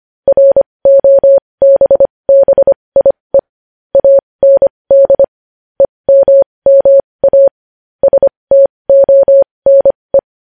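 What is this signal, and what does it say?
Morse code, heard as a keyed tone, about 25 wpm, spaced as for 20 wpm, 550 Hz. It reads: ROBBIE AND EMMA STONE
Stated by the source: Morse code news headlines practice at 20 wpm.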